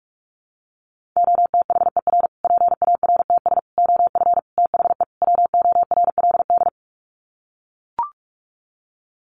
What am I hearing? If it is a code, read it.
Morse code, heard as a keyed tone, about 44 wpm, 700 Hz. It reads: OTHER PARTS OF THE WORLD
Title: Sets of 5 Words Encore 45wpm